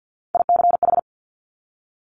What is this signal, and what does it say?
Morse code, heard as a keyed tone, about 50 wpm, 700 Hz. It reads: ICH
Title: Top Three Letter Combinations in Words 50wpm